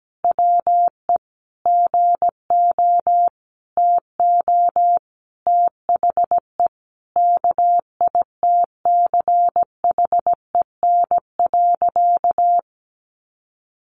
Morse code, heard as a keyed tone, about 17 wpm, 700 Hz. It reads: WE GO TO THE KITCHEN.
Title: Sentences from Top 1400 Words 17wpm